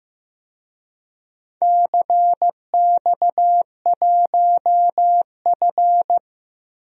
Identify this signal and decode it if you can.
Morse code, heard as a keyed tone, about 15 wpm, 700 Hz. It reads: CX1F
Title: DXCC-100 Call Signs - 4 Characters - 15wpm